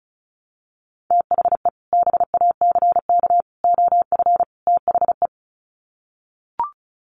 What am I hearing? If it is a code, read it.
Morse code, heard as a keyed tone, about 34 wpm, 700 Hz. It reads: THE BACK OF THE
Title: Sets of 4 Words 35wpm